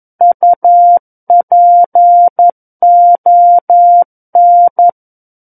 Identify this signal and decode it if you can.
Morse code, heard as a keyed tone, about 11 wpm, 700 Hz. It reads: UPON
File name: 11 WPM